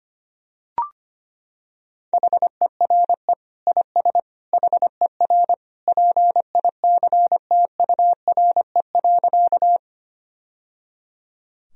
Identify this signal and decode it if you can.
Morse code, heard as a keyed tone, about 25 wpm, 700 Hz. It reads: HERE IS HER PICTURE.
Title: Sentences from Top 1000 Words - Review Part 1 - 25wpm